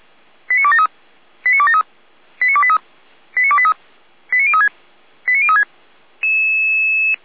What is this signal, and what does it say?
Five-tone selective call (ZVEI), heard as quick runs of five tones, tones of about 70 ms. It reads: 89273 89273 89283 89283 89036 89036
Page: audio.f85360.feuerwehr-orlamuende.org